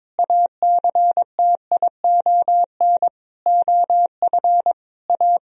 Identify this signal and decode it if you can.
Morse code, heard as a keyed tone, about 22 wpm, 700 Hz. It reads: ACTION OF A
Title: Sentences from Top 1000 Words - Review Part 4 - 22wpm